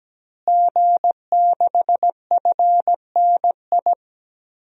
Morse code, heard as a keyed tone, about 17 wpm, 700 Hz. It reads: G6FNI
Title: DXCC-100 Call Signs - 5 Characters - Repeated 3x - 17wpm